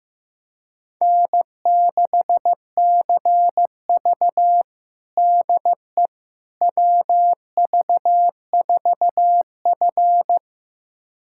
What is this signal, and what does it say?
Morse code, heard as a keyed tone, about 15 wpm, 700 Hz. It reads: N6CV DE WV4F
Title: USA Callsign DE Callsign - 4 Characters - 15wpm